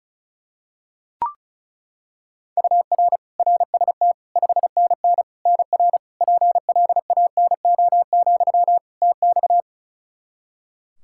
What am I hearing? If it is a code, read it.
Morse code, heard as a keyed tone, about 34 wpm, 700 Hz. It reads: UR RST 5NN NR PLANO, TX
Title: Practice QSO - Exchange 1 - Beginning - 35wpm